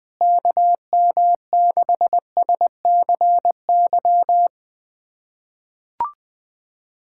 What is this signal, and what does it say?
Morse code, heard as a keyed tone, about 21 wpm, 700 Hz. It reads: KM6SCY
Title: USA Callsign DE Callsign 20wpm